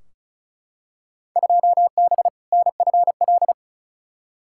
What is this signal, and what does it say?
Morse code, heard as a keyed tone, about 36 wpm, 690 Hz. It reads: 2B NFL
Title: Field Day Exchanges Encore - 35wpm